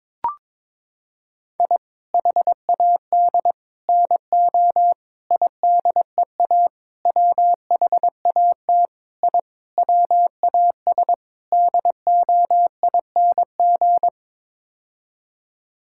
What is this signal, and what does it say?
Morse code, heard as a keyed tone, about 21 wpm, 700 Hz. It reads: I HAD NO IDEA WHAT I WAS DOING
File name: practice / 22wpm / Sets of 8 Words - 22wpm